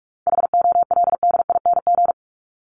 Morse code, heard as a keyed tone, about 44 wpm, 700 Hz. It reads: HOLDING
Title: ING Suffix - Top 100 Words with Base Words 45wpm